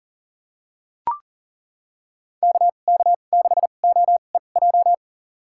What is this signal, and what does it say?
Morse code, heard as a keyed tone, about 40 wpm, spaced as for 29 wpm, 700 Hz. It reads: KK6OEJ
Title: USA Call Signs - 6 Characters - 2x Character Spacing - 40wpm